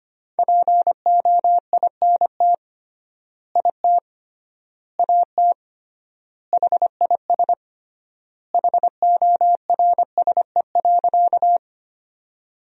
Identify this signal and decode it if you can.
Morse code, heard as a keyed tone, about 25 wpm, 700 Hz. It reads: POINT IT AT HIS HORSE.
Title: 3x Word-Spacing - Sentences from Top 300 Words - 25wpm